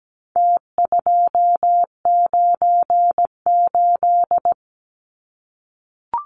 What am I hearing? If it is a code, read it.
Morse code, heard as a keyed tone, about 17 wpm, 700 Hz. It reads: T298
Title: Numbers - Contest Style - 4-Digit 17wpm